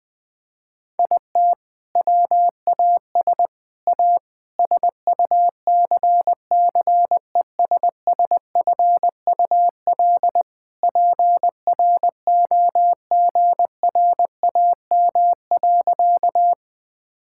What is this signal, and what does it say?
Morse code, heard as a keyed tone, about 20 wpm, 700 Hz. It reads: IT WAS A SUCCESSFUL PROGRAM.
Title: Sentences from Top 1000 Words - Review Part 5 - 20wpm